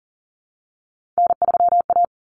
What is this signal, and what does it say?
Morse code, heard as a keyed tone, about 40 wpm, 700 Hz. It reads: N3A
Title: Common Cut Numbers - 3-Digit 40wpm